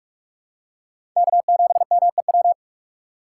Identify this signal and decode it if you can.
Morse code, heard as a keyed tone, about 42 wpm, 700 Hz. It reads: K7MEW
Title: USA Call Signs - 5 Characters - 6x Character Spacing - 45wpm